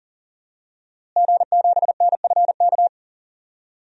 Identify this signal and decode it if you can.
Morse code, heard as a keyed tone, about 40 wpm, 700 Hz. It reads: G7NFK
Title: DXCC-100 Call Signs - 5 Characters - Repeated 3x - 40wpm